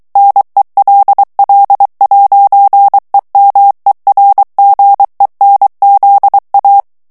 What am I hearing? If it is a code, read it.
Morse code, heard as a keyed tone, about 23 wpm, 800 Hz. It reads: NELL'EMERGENZA
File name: morse mp3 / ITALIEN texte clair 1 vitesse 22 mots minute